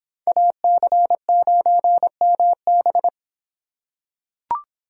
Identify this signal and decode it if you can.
Morse code, heard as a keyed tone, about 26 wpm, 700 Hz. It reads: AC9MB